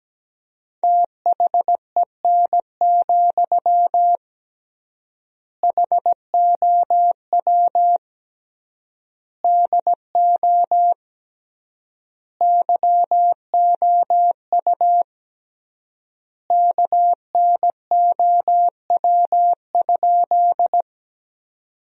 Morse code, heard as a keyed tone, about 17 wpm, 700 Hz. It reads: THEN, HOW DO YOU KNOW?
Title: 3x Word-Spacing - Sentences from Top 100 Words - 17wpm